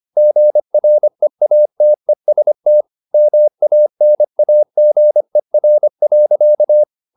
Morse code, heard as a keyed tone, about 25 wpm, 600 Hz. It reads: GREATEST MANAGER.
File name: the old man and the sea part 1 25wpm